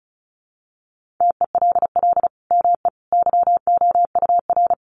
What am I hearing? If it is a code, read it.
Morse code, heard as a keyed tone, about 34 wpm, 700 Hz. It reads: TELL ME YOUR
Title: Sentences from Top 1400 Words 35wpm